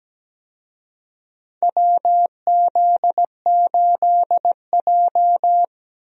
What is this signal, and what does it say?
Morse code, heard as a keyed tone, about 17 wpm, 700 Hz. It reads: WZ8J
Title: USA Call Signs - 4 Characters - Repeated 2x - 17wpm